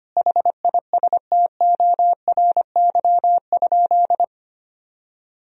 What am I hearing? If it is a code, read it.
Morse code, heard as a keyed tone, about 25 wpm, 700 Hz. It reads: HISTORY?
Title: Sentences from Top 1000 Words - Review Part 4 - 25wpm